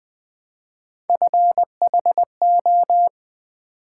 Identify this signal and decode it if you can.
Morse code, heard as a keyed tone, about 20 wpm, 700 Hz. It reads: FHO
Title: Three Letter-Number Combinations - 20wpm